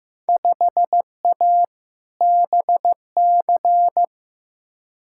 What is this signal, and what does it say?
Morse code, heard as a keyed tone, about 15 wpm, 700 Hz. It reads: 5A BC
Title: Field Day Exchanges - 15wpm